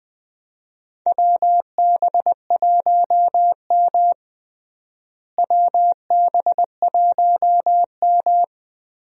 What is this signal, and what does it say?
Morse code, heard as a keyed tone, about 21 wpm, 700 Hz. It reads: WB1M WB1M
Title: USA Call Signs - 4 Characters - Repeated 2x - 20wpm